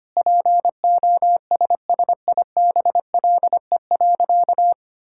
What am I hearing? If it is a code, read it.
Morse code, heard as a keyed tone, about 25 wpm, 700 Hz. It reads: POSSIBLE.
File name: practice / 25wpm / Sentences with 3-Word Phrases - 25wpm